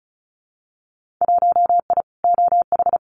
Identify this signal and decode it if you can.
Morse code, heard as a keyed tone, about 34 wpm, 700 Hz. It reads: 1I OH
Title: Winter Field Day Exchanges 35wpm